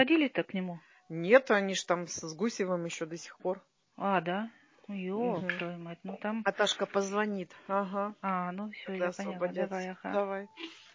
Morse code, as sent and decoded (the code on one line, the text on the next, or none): none